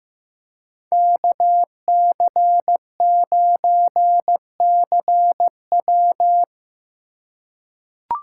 Morse code, heard as a keyed tone, about 16 wpm, 700 Hz. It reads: KC9CW
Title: USA Call Signs - 5 Characters - 4x Character Spacing - 15wpm